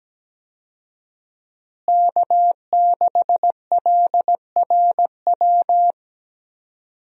Morse code, heard as a keyed tone, about 17 wpm, 700 Hz. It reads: K6LRW
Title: USA Call Signs - 5 Characters - Repeated 2x - 17wpm